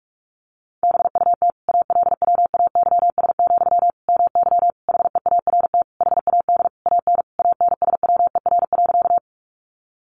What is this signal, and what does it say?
Morse code, heard as a keyed tone, about 44 wpm, 700 Hz. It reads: BUT ALWAYS, MY HEART HAD AN ANSWER.
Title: Sentences from Top 1200 Words 45wpm